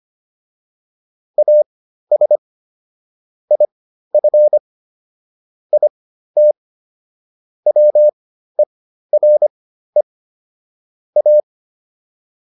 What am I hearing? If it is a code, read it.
Morse code, heard as a keyed tone, about 25 wpm, spaced as for 13 wpm, 600 Hz. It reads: AS IF IT WERE A